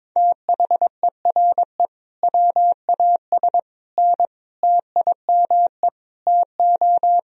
Morse code, heard as a keyed tone, about 21 wpm, 700 Hz. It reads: THERE WAS N TIME TO
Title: Sentences from Top 400 Words and Abbreviations 22wpm